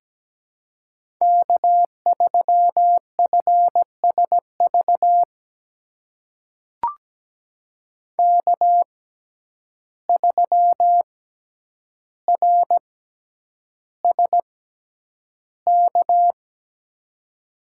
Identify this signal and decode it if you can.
Morse code, heard as a keyed tone, about 17 wpm, spaced as for 9 wpm, 700 Hz. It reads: K3FSV K3RSK
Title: USA Call Signs - 5 Characters - 6x Character Spacing - 17wpm